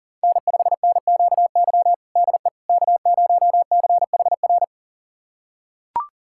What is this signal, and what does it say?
Morse code, heard as a keyed tone, about 40 wpm, 700 Hz. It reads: N5NQY DE K0CHR
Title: USA Callsign DE Callsign - 5 Characters - 40wpm